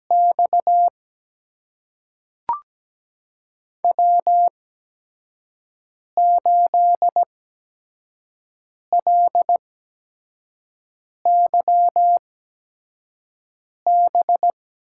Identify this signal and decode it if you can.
Morse code, heard as a keyed tone, about 17 wpm, spaced as for 5 wpm, 700 Hz. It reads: X W8LYB
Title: USA Call Signs - 5 Characters - 8x Character Spacing - 17wpm